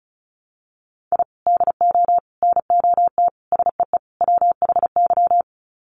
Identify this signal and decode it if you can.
Morse code, heard as a keyed tone, about 34 wpm, 700 Hz. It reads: I DO NOT SEE WHY
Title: Sets of 5 Words Encore 35wpm